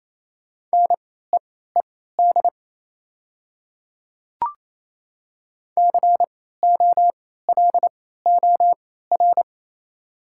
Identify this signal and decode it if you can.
Morse code, heard as a keyed tone, about 28 wpm, spaced as for 16 wpm, 700 Hz. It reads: NEED COLOR
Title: 3x Character Spacing - Top 300 Words - 28wpm